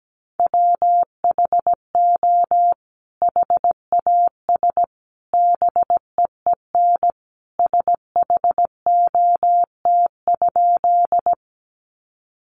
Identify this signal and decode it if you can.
Morse code, heard as a keyed tone, about 17 wpm, 700 Hz. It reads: WHO HAS BEEN SHOT?
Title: Sentences from Top 1100 Words 17wpm